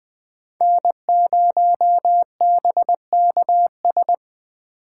Morse code, heard as a keyed tone, about 20 wpm, 700 Hz. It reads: N0BKS
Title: Speed Racing - USA 1x3 Call Signs 17wpm